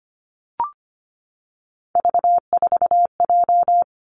Morse code, heard as a keyed tone, about 25 wpm, 700 Hz. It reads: V4J
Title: Three Challenging Letter-Number Combinations 25wpm